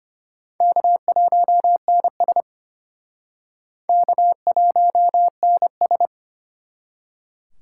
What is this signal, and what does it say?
Morse code, heard as a keyed tone, about 32 wpm, 700 Hz. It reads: K1NS K1NS